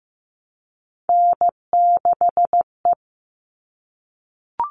Morse code, heard as a keyed tone, about 15 wpm, 700 Hz. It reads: N6E
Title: All Cut Numbers - 3-Digit 15wpm